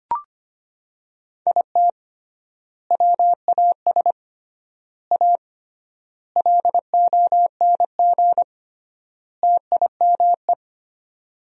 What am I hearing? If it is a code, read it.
Morse code, heard as a keyed tone, about 25 wpm, 700 Hz. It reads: IT WAS A LONG TIME
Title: 3x Word-Spacing - Sentences from Top 500 Words - 25wpm